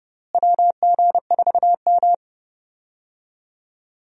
Morse code, heard as a keyed tone, about 30 wpm, 700 Hz. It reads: WG4M
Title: USA Call Signs - 4 Characters - Repeated 2x - 30wpm